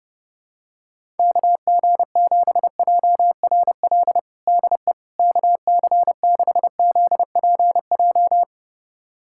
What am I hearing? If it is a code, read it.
Morse code, heard as a keyed tone, about 30 wpm, 700 Hz. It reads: KG7JRL DE KC6ZPJ
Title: USA Callsign DE Callsign - 6 Characters - 30wpm